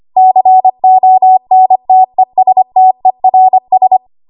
Morse code, heard as a keyed tone, about 25 wpm, 750 Hz. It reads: CONTESTERS
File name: Archive / 25 WPM / 25WPM